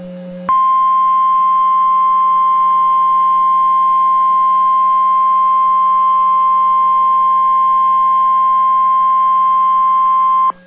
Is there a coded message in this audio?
Yes: Emergency Alert System (SAME) tones (the 1050 Hz attention tone) from 0.9 to 10.4 s.